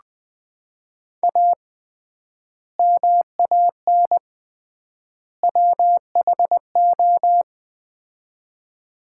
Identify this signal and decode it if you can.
Morse code, heard as a keyed tone, about 21 wpm, 700 Hz. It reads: A MAN WHO
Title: 3x Word-Spacing - Sets of 3 Words - 20wpm